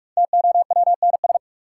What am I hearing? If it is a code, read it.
Morse code, heard as a keyed tone, about 43 wpm, 700 Hz. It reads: TOWNS
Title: Sentences from Top 1000 Words - Review Part 6 - New Word Drill - 45wpm